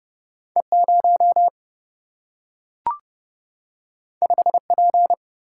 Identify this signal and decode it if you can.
Morse code, heard as a keyed tone, about 30 wpm, 700 Hz. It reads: E0 5P
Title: Single-Letter Single-Number 30wpm